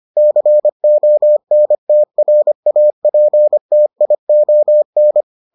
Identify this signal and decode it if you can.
Morse code, heard as a keyed tone, about 25 wpm, 600 Hz. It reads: CONTRAPTION